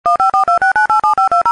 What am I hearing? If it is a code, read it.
DTMF, heard as telephone touch tones, tones of about 130 ms. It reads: 15426987520